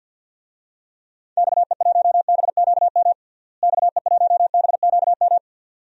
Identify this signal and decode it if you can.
Morse code, heard as a keyed tone, about 53 wpm, 700 Hz. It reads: XE1BQM XE1BQM
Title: DXCC-100 Call Signs - 6 Characters - Repeated 2x - 50wpm